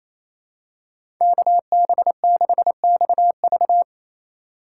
Morse code, heard as a keyed tone, about 28 wpm, 700 Hz. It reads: KB6XV
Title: USA Call Signs - 3x Character Spacing - 28wpm